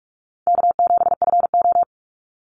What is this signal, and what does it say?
Morse code, heard as a keyed tone, about 44 wpm, 700 Hz. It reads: K7FO
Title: NA SOTA Chaser Callsigns 45wpm